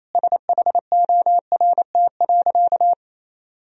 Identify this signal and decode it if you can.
Morse code, heard as a keyed tone, about 28 wpm, 700 Hz. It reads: SHORT.
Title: Sentences from Top 500 Words and Abbreviations 28wpm